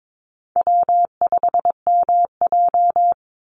Morse code, heard as a keyed tone, about 22 wpm, 700 Hz. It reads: W5MJ
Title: Easy Call Signs Repeated 2x - 22wpm